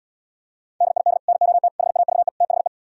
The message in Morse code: DID NOT HAVE THE

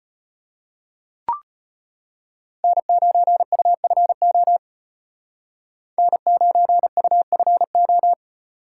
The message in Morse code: N9UFO N9UFO

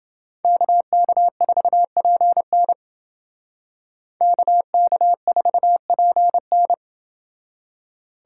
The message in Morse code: KK4PN KK4PN

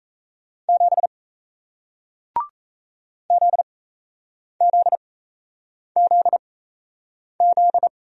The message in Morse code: 7 ZZZZ

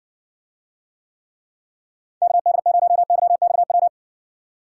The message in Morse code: KD0YXW

